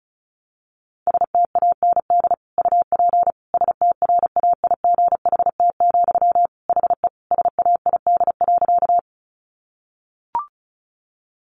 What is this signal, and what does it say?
Morse code, heard as a keyed tone, about 34 wpm, 700 Hz. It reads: STAND UP STRAIGHT, HE SAID.
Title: Sentences from Top 1000 Words 35wpm